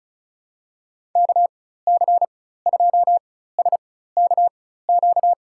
Morse code, heard as a keyed tone, about 34 wpm, spaced as for 16 wpm, 700 Hz. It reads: KC2SKQ
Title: USA Call Signs - 6 Characters - 4x Character Spacing - 35wpm